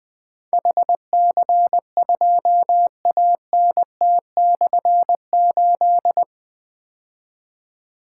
Morse code, heard as a keyed tone, about 20 wpm, 700 Hz. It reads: HC2ANT/8
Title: Difficult Call Signs 20wpm